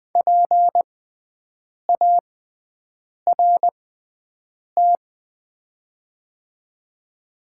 Morse code, heard as a keyed tone, about 20 wpm, spaced as for 7 wpm, 700 Hz. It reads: PART
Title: 6x Character Spacing - Top 200 Words - 20wpm